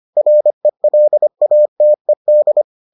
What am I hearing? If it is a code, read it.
Morse code, heard as a keyed tone, about 25 wpm, 600 Hz. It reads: RELATED